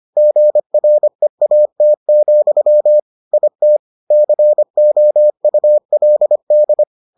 Morse code, heard as a keyed tone, about 25 wpm, 600 Hz. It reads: GREAT, IT COULD